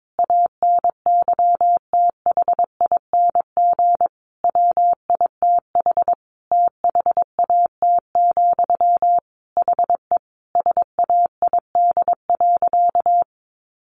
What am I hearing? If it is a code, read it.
Morse code, heard as a keyed tone, about 22 wpm, 700 Hz. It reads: ANYTHING WITH THAT, HE SAID.
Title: Sentences from Top 1000 Words 22wpm